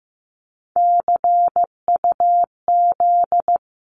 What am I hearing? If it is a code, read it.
Morse code, heard as a keyed tone, about 15 wpm, 700 Hz. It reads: CUZ
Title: Abbreviations 15wpm